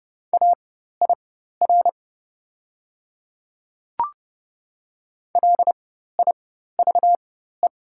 Morse code, heard as a keyed tone, about 30 wpm, spaced as for 14 wpm, 690 Hz. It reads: AIR LIVE
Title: 4x Character Spacing - Top 200 Words - 30wpm